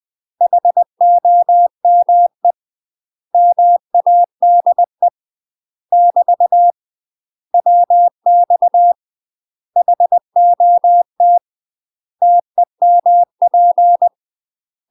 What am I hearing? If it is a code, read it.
Morse code, heard as a keyed tone, about 20 wpm, 700 Hz. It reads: HOME MADE <BT> WX HOT TEMP
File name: UK MASTER QSOs TRAINING / 20WPM / UK MASTER QSOs 0023